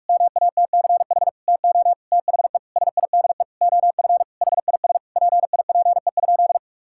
Morse code, heard as a keyed tone, about 46 wpm, 700 Hz. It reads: MATCH TO THE SIDE OF HIS PIPE?
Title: Sentences from Top 1600 Words 45wpm